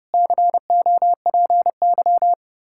Morse code, heard as a keyed tone, about 30 wpm, 700 Hz. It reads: COPY
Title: Practice QSO - Exchange 3 - More Info - 30wpm